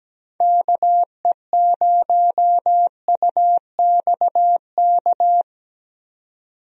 Morse code, heard as a keyed tone, about 17 wpm, 700 Hz. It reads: KE0UXK
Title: USA Call Signs - 6 Characters - 17wpm